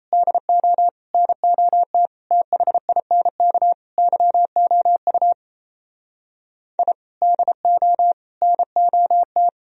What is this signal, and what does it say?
Morse code, heard as a keyed tone, about 34 wpm, 700 Hz. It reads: DO NOT THINK YOU I DO NOT
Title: Speed-Racing - Sets of 5 Words 28wpm